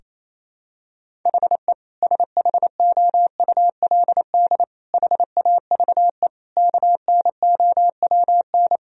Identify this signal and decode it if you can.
Morse code, heard as a keyed tone, about 28 wpm, 700 Hz. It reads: HE SHOULD HAVE KNOWN
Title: Sentences from Top 1000 Words - Review Part 3 - 28wpm